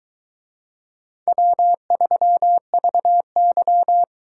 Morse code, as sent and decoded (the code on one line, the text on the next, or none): .-- ...-- ...- -.--
W3VY